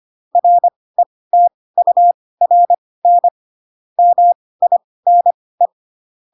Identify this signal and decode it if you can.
Morse code, heard as a keyed tone, about 25 wpm, spaced as for 18 wpm, 700 Hz. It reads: RETURN MINE
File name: Pinocchio / Pinocchio ch2 12 25 ik1hge